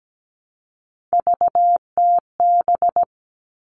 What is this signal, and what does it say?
Morse code, heard as a keyed tone, about 17 wpm, 700 Hz. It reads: VTB
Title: All Cut Numbers - 3-Digit 17wpm